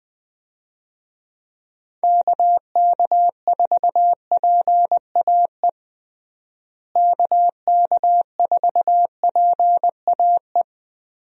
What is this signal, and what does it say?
Morse code, heard as a keyed tone, about 21 wpm, 700 Hz. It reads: KK4PAE KK4PAE
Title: USA Call Signs - 6 Characters - Repeated 3x - 20wpm